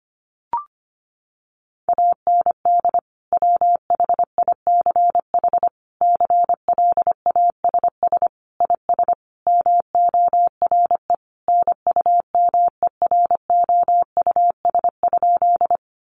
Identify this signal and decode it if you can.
Morse code, heard as a keyed tone, about 25 wpm, 700 Hz. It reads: AND WHICH CLASS IS MORE NUMEROUS?
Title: Sentences from Top 1400 Words 25wpm